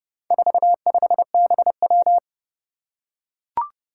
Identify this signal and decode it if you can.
Morse code, heard as a keyed tone, about 29 wpm, 690 Hz. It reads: 45BW
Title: Four Challenging Letter-Number Combinations 30wpm